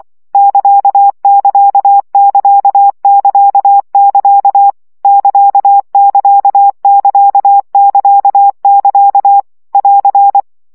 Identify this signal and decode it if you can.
Morse code, heard as a keyed tone, about 24 wpm, 800 Hz. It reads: <KA><KA><KA><KA><KA> <KA><KA><KA><KA><KA> <AR>
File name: CW gyakorlas ct 20 baud